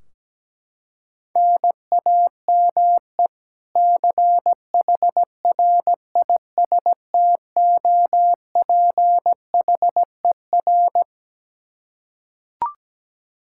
Morse code, heard as a keyed tone, about 17 wpm, 700 Hz. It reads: NAME CHRISTOPHER
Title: Practice QSO - Exchange 1 - Beginning - 17wpm